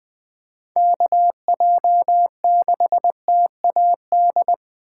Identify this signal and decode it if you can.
Morse code, heard as a keyed tone, about 20 wpm, 700 Hz. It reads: KJ6TAD
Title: USA Call Signs - 6 Characters - 4x Character Spacing - 20wpm